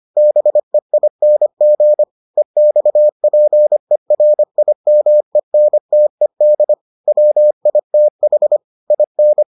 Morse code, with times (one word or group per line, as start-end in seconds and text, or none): 0.2-2.0 BEING
2.4-6.7 EXPERIMENTED
7.1-8.6 WITH
8.9-9.4 IN